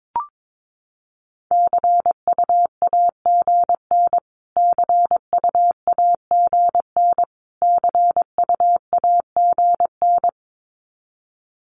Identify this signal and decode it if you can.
Morse code, heard as a keyed tone, about 22 wpm, 700 Hz. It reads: CUAGN CUAGN CUAGN
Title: Abbreviations Repeated 3x - 22wpm